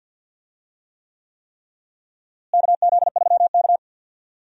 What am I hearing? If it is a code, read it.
Morse code, heard as a keyed tone, about 53 wpm, 700 Hz. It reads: KZ3K
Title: USA Call Signs - 4 Characters - Repeated 3x - 50wpm